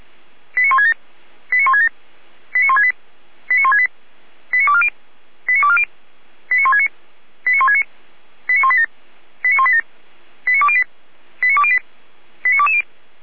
Five-tone selective call (ZVEI), heard as quick runs of five tones, tones of about 70 ms. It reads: 89167 89167 89168 89168 89240 89240 89169 89169 89187 89187 89208 89208 89220